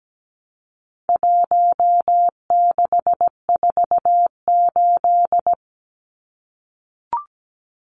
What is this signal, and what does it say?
Morse code, heard as a keyed tone, about 17 wpm, 700 Hz. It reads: 1648
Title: Numbers - 4-Digit 17wpm